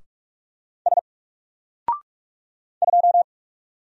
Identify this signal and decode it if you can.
Morse code, heard as a keyed tone, about 46 wpm, 700 Hz. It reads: S 2
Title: Single Letter-Number 45wpm